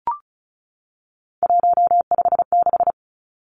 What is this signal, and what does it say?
Morse code, heard as a keyed tone, about 36 wpm, 700 Hz. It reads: 156
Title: Three Challenging Letter-Number Combinations 35wpm